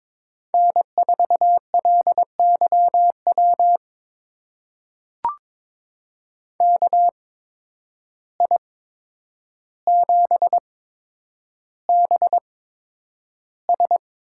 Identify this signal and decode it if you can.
Morse code, heard as a keyed tone, about 22 wpm, spaced as for 9 wpm, 700 Hz. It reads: N4LYW KI7BS
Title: USA Call Signs - 8x Character Spacing - 22wpm